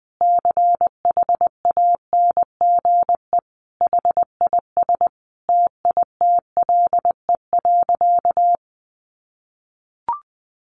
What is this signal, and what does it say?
Morse code, heard as a keyed tone, about 20 wpm, 700 Hz. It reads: CHANGE HIS TITLE.
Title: Sentences from Top 1200 Words 20wpm